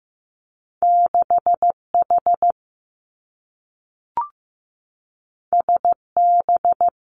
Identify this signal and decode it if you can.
Morse code, heard as a keyed tone, about 15 wpm, 700 Hz. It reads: 6 H S B
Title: Two Challenging Letter-Number Combinations 15wpm